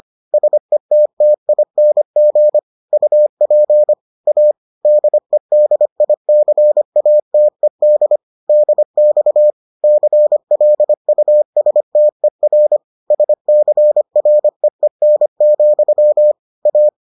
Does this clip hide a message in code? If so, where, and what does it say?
Yes: Morse code from 0.3 to 16.9 s, heard as a keyed tone, about 25 wpm, 600 Hz. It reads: SETTING UP A DEDICATED DX CLUSTER SCREEN, A